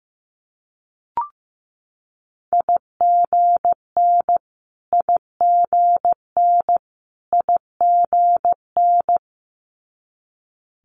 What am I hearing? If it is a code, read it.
Morse code, heard as a keyed tone, about 15 wpm, 700 Hz. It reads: IGN IGN IGN
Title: Multi-letter Phonemes Repeated 3x 15wpm